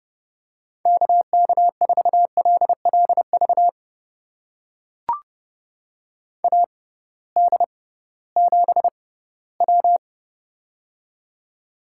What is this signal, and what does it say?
Morse code, heard as a keyed tone, about 30 wpm, spaced as for 18 wpm, 700 Hz. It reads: KK4LLV AD7W
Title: USA Call Signs - 6x Character Spacing - 30wpm